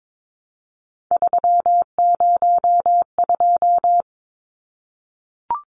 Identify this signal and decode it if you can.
Morse code, heard as a keyed tone, about 22 wpm, 700 Hz. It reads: 302
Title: Numbers - 3-Digit 22wpm